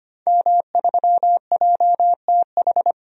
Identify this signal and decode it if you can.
Morse code, heard as a keyed tone, about 25 wpm, 700 Hz. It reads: M3JTH
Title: DXCC-100 Call Signs - 5 Characters - 25wpm